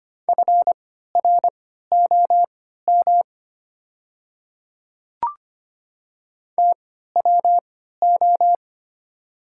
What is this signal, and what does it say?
Morse code, heard as a keyed tone, about 25 wpm, spaced as for 14 wpm, 700 Hz. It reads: FROM TWO
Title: 3x Character Spacing - Top 100 Words - 25wpm